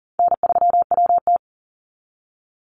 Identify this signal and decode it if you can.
Morse code, heard as a keyed tone, about 40 wpm, 700 Hz. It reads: N3WT
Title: Easy Call Signs Repeated 3x - 40wpm